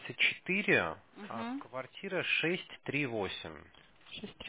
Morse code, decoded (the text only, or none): none